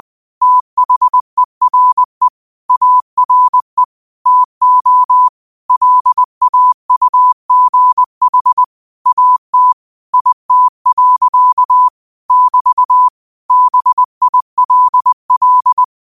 Morse code, heard as a keyed tone, about 20 wpm, 1000 Hz. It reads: THERE ARE TO LAUGH AT IT. <BT> BILL
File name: qotdfiles / 20wpm